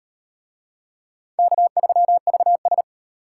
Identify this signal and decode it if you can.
Morse code, heard as a keyed tone, about 35 wpm, 700 Hz. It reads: K3VS